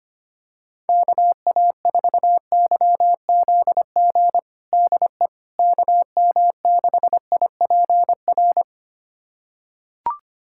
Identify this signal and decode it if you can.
Morse code, heard as a keyed tone, about 25 wpm, 700 Hz. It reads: KA4YZG DE KM6IPR
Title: USA Callsign DE Callsign - 6 Characters - 25wpm